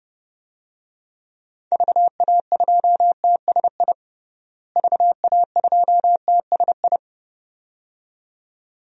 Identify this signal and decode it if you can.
Morse code, heard as a keyed tone, about 30 wpm, 700 Hz. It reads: VA2TSI VA2TSI
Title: DXCC-100 Call Signs - 6 Characters - Repeated 2x - 30wpm